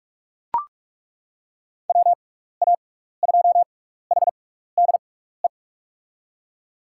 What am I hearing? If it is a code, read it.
Morse code, heard as a keyed tone, about 41 wpm, spaced as for 15 wpm, 700 Hz. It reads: WA2HDE
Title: USA Call Signs - 6 Characters - 6x Character Spacing - 45wpm